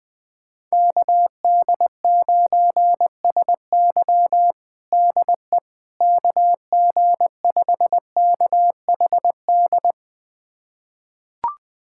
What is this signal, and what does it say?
Morse code, heard as a keyed tone, about 20 wpm, 700 Hz. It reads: KD9SY DE KG5KHD
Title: USA Callsign DE Callsign 20wpm